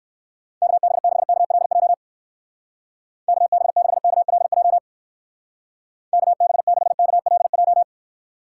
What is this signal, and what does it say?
Morse code, heard as a keyed tone, about 59 wpm, 700 Hz. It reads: KB6KLJ KB6KLJ KB6KLJ